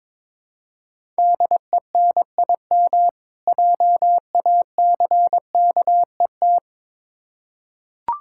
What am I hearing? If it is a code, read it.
Morse code, heard as a keyed tone, about 21 wpm, 700 Hz. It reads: DENIM JACKET
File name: Difficult Words with Hints - 22wpm